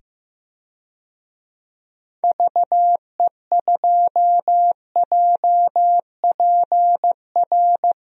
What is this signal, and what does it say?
Morse code, heard as a keyed tone, about 15 wpm, 700 Hz. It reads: VE2JPR
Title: DXCC-100 Call Signs - 6 Characters - 15wpm